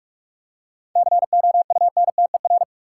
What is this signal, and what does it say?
Morse code, heard as a keyed tone, about 41 wpm, 700 Hz. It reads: COUNTER
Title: Seven Letter Words - 45wpm